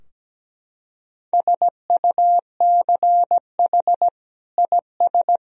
Morse code, heard as a keyed tone, about 17 wpm, 700 Hz. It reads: SUCH IS
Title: Sentences from Top 1000 Words - Review Part 2 - 17wpm